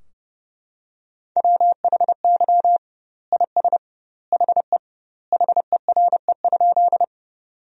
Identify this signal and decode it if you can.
Morse code, heard as a keyed tone, about 32 wpm, 700 Hz. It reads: WHY IS HE HERE?
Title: 2x Word-Spacing - Sentences from Top 200 Words - 30wpm